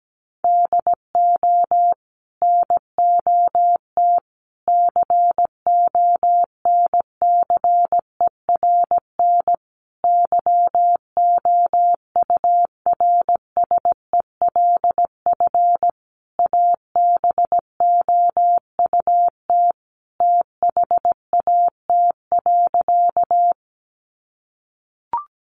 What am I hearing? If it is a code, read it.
Morse code, heard as a keyed tone, about 17 wpm, 700 Hz. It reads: DO NOT CONCERN YOURSELF ABOUT THAT.
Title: Sentences from Top 1000 Words 17wpm